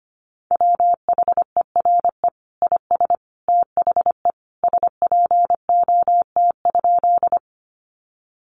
Morse code, heard as a keyed tone, about 25 wpm, 700 Hz. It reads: WHERE IS THE SPOT?
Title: Sentences from Top 1100 Words 25wpm